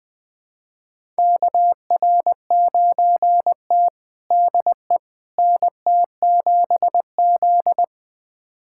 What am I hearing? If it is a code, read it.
Morse code, heard as a keyed tone, about 20 wpm, 700 Hz. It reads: KR9T DE NT7Z
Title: USA Callsign DE Callsign - 4 Characters - 20wpm